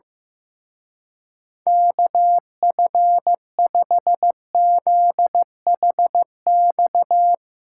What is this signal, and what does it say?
Morse code, heard as a keyed tone, about 15 wpm, 700 Hz. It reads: KF5ZHX
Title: Speed Racing - USA 2x3 Call Signs 15wpm